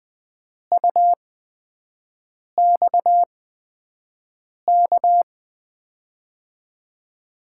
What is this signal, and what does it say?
Morse code, heard as a keyed tone, about 20 wpm, spaced as for 5 wpm, 700 Hz. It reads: UXK
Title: USA Call Signs - 8x Character Spacing - 20wpm